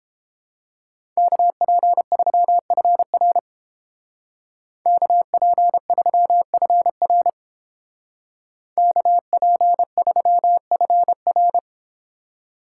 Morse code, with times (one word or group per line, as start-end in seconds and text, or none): 1.2-3.4 KP3FR
4.9-7.3 KP3FR
8.8-11.6 KP3FR